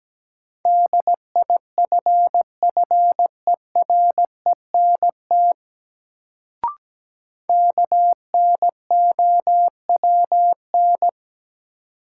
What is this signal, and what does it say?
Morse code, heard as a keyed tone, about 17 wpm, 700 Hz. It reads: DIFFERENT KNOWN
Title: Sentences from Top 600 Words - New Word Drill - 17wpm